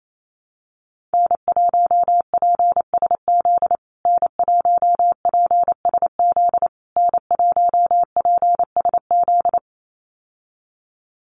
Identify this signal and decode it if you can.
Morse code, heard as a keyed tone, about 28 wpm, 700 Hz. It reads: N1PSZ N1PSZ N1PSZ
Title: USA Call Signs Repeated 3x 28wpm